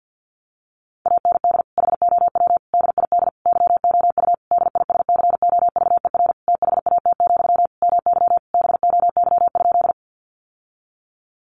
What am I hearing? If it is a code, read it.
Morse code, heard as a keyed tone, about 48 wpm, 700 Hz. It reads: AND HOW DID YOU DISCOVER THAT, MY BOY?
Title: Sentences from Top 1100 Words 50wpm